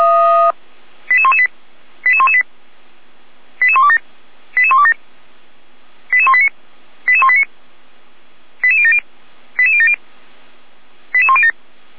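Five-tone selective call (ZVEI), heard as quick runs of five tones, tones of about 70 ms. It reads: 88118 88118 88127 88127 88189 88189 88070 88070 88107
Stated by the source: ZVEI five-tone selective call received by SDR, logged by feuerwehr-orlamuende.org (Feuerwehr Orlamünde fire service).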